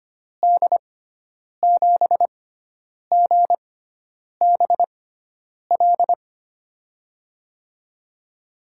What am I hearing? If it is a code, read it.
Morse code, heard as a keyed tone, about 25 wpm, spaced as for 9 wpm, 700 Hz. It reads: D7GBL
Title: USA Call Signs - 6 Characters - 6x Character Spacing - 25wpm